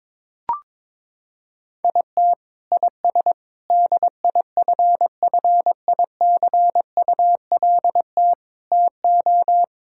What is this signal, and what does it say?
Morse code, heard as a keyed tone, about 22 wpm, 700 Hz. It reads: IT IS DIFFICULT TO
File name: practice / 22wpm / Sentences with 4-Word Phrases - 22wpm